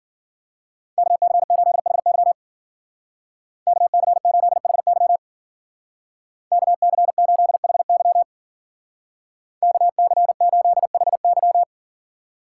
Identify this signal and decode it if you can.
Morse code, heard as a keyed tone, about 50 wpm, 700 Hz. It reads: KC8HY KC8HY KC8HY KC8HY